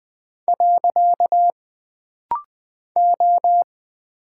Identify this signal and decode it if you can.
Morse code, heard as a keyed tone, about 20 wpm, 700 Hz. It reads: . O
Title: Single Character (Letter:Number:Punctuation:Prosign) - Rapid-Fire - 20wpm